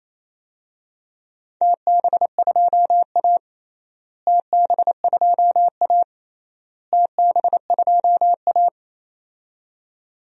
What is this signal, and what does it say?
Morse code, heard as a keyed tone, about 28 wpm, 700 Hz. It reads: TB2A TB2A TB2A
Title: DXCC-100 Call Signs - 4 Characters - Repeated 3x - 28wpm